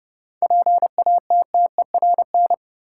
Morse code, heard as a keyed tone, about 30 wpm, 700 Hz. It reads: PATTERN
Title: Seven Letter Words - 30wpm